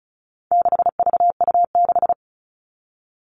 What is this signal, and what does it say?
Morse code, heard as a keyed tone, about 34 wpm, 700 Hz. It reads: 6VU6
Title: All Cut Numbers - 4-Digit 35wpm